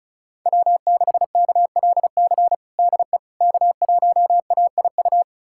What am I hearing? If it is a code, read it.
Morse code, heard as a keyed tone, about 36 wpm, 700 Hz. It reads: W6KLC DE K1AIU
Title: USA Callsign DE Callsign - 5 Characters - 35wpm